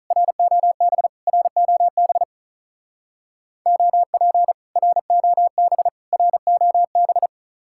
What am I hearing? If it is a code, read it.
Morse code, heard as a keyed tone, about 47 wpm, 700 Hz. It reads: ROB ROB OP ROB ROB